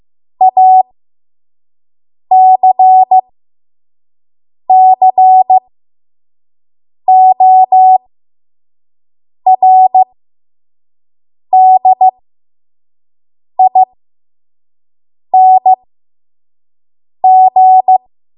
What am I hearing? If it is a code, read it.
Morse code, heard as a keyed tone, about 15 wpm, spaced as for 5 wpm, 750 Hz. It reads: ACCORDING
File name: Morse / 05WPM